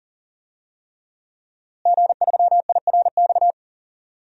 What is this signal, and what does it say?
Morse code, heard as a keyed tone, about 40 wpm, 700 Hz. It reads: G3IRX
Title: DXCC-100 Call Signs - 5 Characters - Repeated 3x - 40wpm